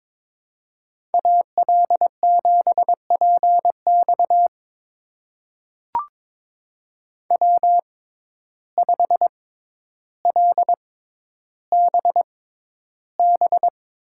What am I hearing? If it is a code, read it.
Morse code, heard as a keyed tone, about 22 wpm, spaced as for 11 wpm, 700 Hz. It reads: AL7PX W5LBB
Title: USA Call Signs - 5 Characters - 6x Character Spacing - 22wpm